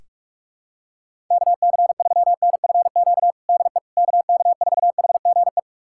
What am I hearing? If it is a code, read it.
Morse code, heard as a keyed tone, about 41 wpm, 700 Hz. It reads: KC3NFQ DE KK4HGE
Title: USA Callsign DE Callsign - 6 Characters - 45wpm